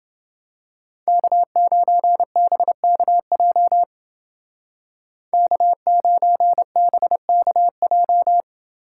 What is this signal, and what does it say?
Morse code, heard as a keyed tone, about 29 wpm, 700 Hz. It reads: K9BKJ K9BKJ